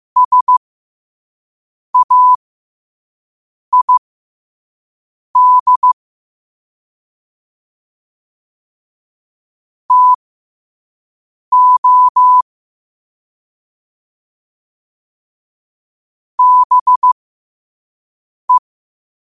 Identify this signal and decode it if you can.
Morse code, heard as a keyed tone, about 15 wpm, spaced as for 5 wpm, 1000 Hz. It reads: SAID TO BE
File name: qotdfiles / 5wpm